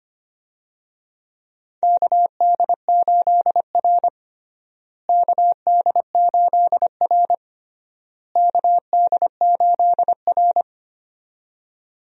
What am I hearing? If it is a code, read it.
Morse code, heard as a keyed tone, about 25 wpm, 700 Hz. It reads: KD8R KD8R KD8R
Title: USA Call Signs - 4 Characters - Repeated 3x - 25wpm